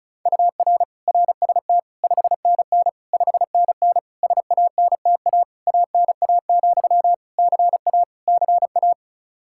Morse code, heard as a keyed tone, about 36 wpm, 700 Hz. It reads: UR RST 5NN 5NN SANTA ANA, CA CA
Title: Practice QSO - Exchange 1 - Beginning - 35wpm